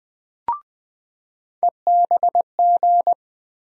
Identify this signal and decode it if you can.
Morse code, heard as a keyed tone, about 20 wpm, 700 Hz. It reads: EBG
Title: USA Call Sign Postfixes - 3 Letters 20wpm